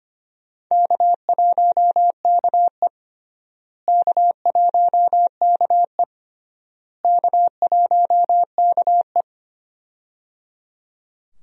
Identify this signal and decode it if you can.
Morse code, heard as a keyed tone, about 25 wpm, 700 Hz. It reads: K1KE K1KE K1KE
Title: USA Call Signs - 4 Characters - Repeated 3x - 25wpm